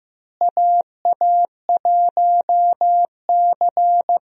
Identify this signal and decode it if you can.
Morse code, heard as a keyed tone, about 15 wpm, 700 Hz. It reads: AA1C
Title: USA Call Signs - 4 Characters - Repeated 2x - 15wpm